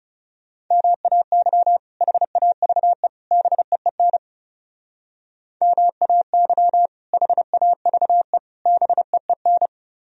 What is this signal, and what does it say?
Morse code, heard as a keyed tone, about 39 wpm, 700 Hz. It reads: MAY HAVE BEEN MAY HAVE BEEN